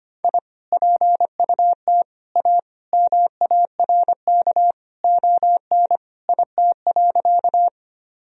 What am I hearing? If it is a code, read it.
Morse code, heard as a keyed tone, about 25 wpm, 700 Hz. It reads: I PUT A MARK ON IT.
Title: Sentences from Top 1000 Words - Review Part 1 - 25wpm